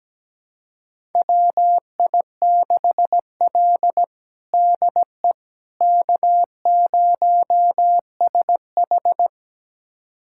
Morse code, heard as a keyed tone, about 17 wpm, 700 Hz. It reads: WI6L DE K0SH
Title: USA Callsign DE Callsign - 4 Characters - 17wpm